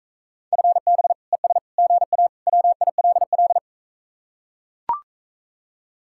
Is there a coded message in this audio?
Yes: Morse code from 0.5 to 3.6 s, heard as a keyed tone, about 43 wpm, 700 Hz. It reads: FB ES GA WILL